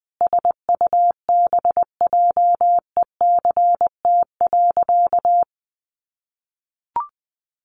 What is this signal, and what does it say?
Morse code, heard as a keyed tone, about 20 wpm, 700 Hz. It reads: SUBJECT.
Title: Sentences from Top 1400 Words 20wpm